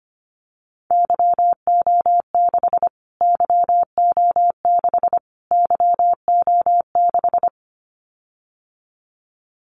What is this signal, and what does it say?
Morse code, heard as a keyed tone, about 25 wpm, 700 Hz. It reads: YO6 YO6 YO6
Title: Callsign Prefixes for Top 25 DXCC - Repeated 3x - 25wpm